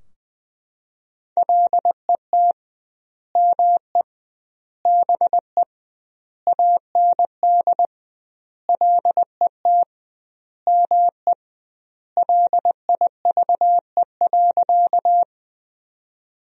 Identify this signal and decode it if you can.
Morse code, heard as a keyed tone, about 20 wpm, 700 Hz. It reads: LET ME BE AND LET ME LIVE.
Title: 2x Word-Spacing - Sentences from Top 300 Words - 20wpm